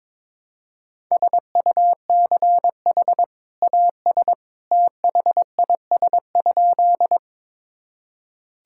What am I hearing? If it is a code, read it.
Morse code, heard as a keyed tone, about 21 wpm, 700 Hz. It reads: SUCH AS THIS?
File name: practice / 22wpm / Sentences with 2-Word Phrases - 22wpm